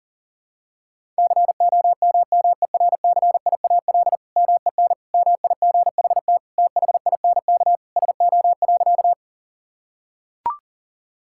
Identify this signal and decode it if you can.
Morse code, heard as a keyed tone, about 40 wpm, 700 Hz. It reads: COMMERCIAL MEN MIGHT THINK SO.
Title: Sentences from Top 1000 Words - Review Part 5 - 40wpm